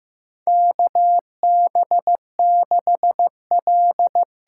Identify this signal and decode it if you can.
Morse code, heard as a keyed tone, about 15 wpm, 700 Hz. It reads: KB6L
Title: USA Call Signs - 6 Characters - 8x Character Spacing - 15wpm